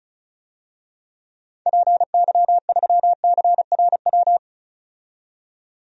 Morse code, heard as a keyed tone, about 34 wpm, 700 Hz. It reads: PY3CRW
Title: DXCC-100 Call Signs - 6 Characters - 35wpm